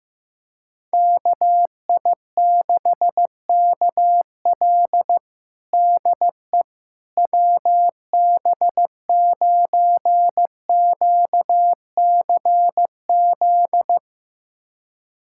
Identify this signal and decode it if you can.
Morse code, heard as a keyed tone, about 15 wpm, 700 Hz. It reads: KI6KL DE WB9QCZ
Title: USA Callsign DE Callsign 15wpm